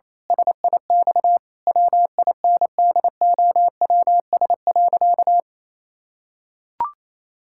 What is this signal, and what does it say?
Morse code, heard as a keyed tone, about 28 wpm, 700 Hz. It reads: SIX WINDOWS.
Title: Pangram Challenge v2 - 28wpm